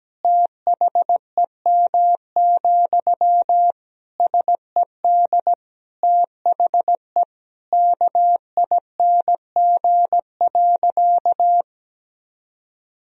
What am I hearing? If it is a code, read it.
Morse code, heard as a keyed tone, about 17 wpm, 700 Hz. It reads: THEM, SED THE KING.
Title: Sentences from Top 400 Words and Abbreviations 17wpm